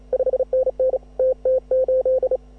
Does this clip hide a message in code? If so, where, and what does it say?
Yes: Morse code from 0.1 to 2.4 s, heard as a keyed tone, about 34 wpm, 560 Hz. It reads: 5NN TT8